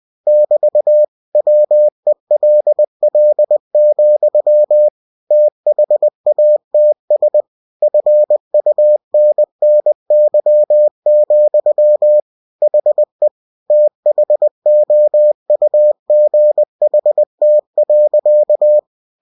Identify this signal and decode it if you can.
Morse code, heard as a keyed tone, about 20 wpm, 600 Hz. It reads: <BT> WELL, THATS FUNNY, HE THOUGHT.
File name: cw-books / winnie-the-pooh / winnie-20wpm-page-0053